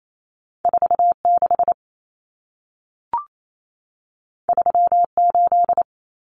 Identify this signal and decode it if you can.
Morse code, heard as a keyed tone, about 28 wpm, 700 Hz. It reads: 46 38
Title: Numbers - 2-Digit 28wpm